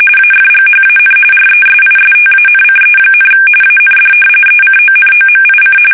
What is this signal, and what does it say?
RTTY, about 45.45 baud. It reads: AFGHAN RESISTANCE, AND ECONOMIC
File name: fsk850 / battleshipnj